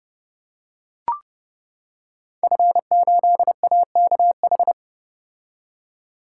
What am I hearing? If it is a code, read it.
Morse code, heard as a keyed tone, about 29 wpm, 700 Hz. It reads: F8AKH
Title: Code Groups - 5 Letter:Numbers - 30wpm